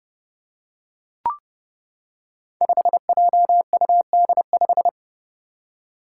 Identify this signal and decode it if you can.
Morse code, heard as a keyed tone, about 30 wpm, 700 Hz. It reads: 5JUD5
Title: Five Challenging Letter-Number Combinations 30wpm